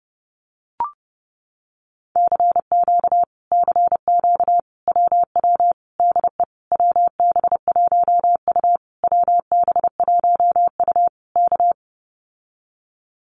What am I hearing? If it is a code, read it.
Morse code, heard as a keyed tone, about 30 wpm, 700 Hz. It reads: CQ CQ WW DE WB1U WB1U K
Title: Common QSO Phrases Encore 30wpm